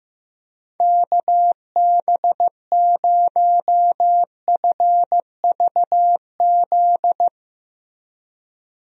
Morse code, heard as a keyed tone, about 15 wpm, 700 Hz. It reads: KB0FVZ
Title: USA Call Signs - 6 Characters - Repeated 3x - 15wpm